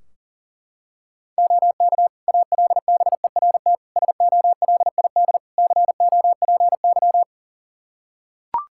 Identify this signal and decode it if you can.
Morse code, heard as a keyed tone, about 40 wpm, 700 Hz. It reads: OK ALBERT SOLID COPY
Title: Practice QSO - Exchange 3 - More Info - 40wpm